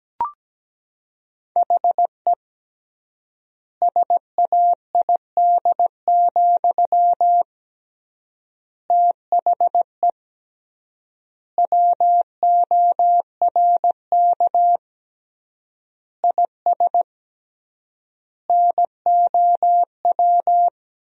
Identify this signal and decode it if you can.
Morse code, heard as a keyed tone, about 17 wpm, 700 Hz. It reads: HE SAID, THE WORK IS NOW
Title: 3x Word-Spacing - Sentences from Top 400 Words - 17wpm